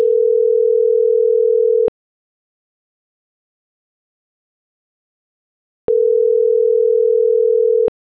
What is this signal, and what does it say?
Ringback tone (North America), 440 + 480 Hz.